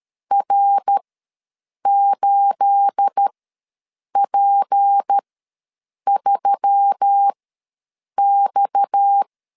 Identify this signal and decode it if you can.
Morse code, heard as a keyed tone, about 13 wpm, spaced as for 7 wpm, 800 Hz. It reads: R8P3X
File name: LEZIONI / 9 28 ws 12wpm 3354